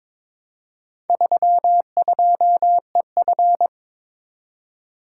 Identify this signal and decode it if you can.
Morse code, heard as a keyed tone, about 22 wpm, 700 Hz. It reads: 32EF
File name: practice / 22wpm / Four Letter-Number Combinations - 22wpm